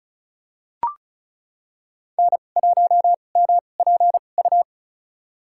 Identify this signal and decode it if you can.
Morse code, heard as a keyed tone, about 36 wpm, spaced as for 25 wpm, 700 Hz. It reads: N1MPU
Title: USA Call Signs - 5 Characters - 2x Character Spacing - 35wpm